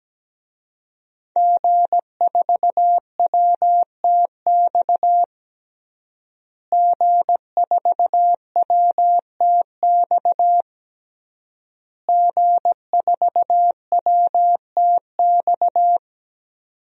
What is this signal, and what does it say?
Morse code, heard as a keyed tone, about 17 wpm, 700 Hz. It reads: G4WTX G4WTX G4WTX
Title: DXCC-100 Call Signs - 5 Characters - Repeated 3x - 17wpm